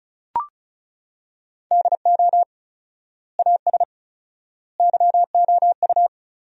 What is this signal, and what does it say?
Morse code, heard as a keyed tone, about 36 wpm, 700 Hz. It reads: DO AS YOU
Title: 4x Word-Spacing - Sentences from Top 100 Words - 35wpm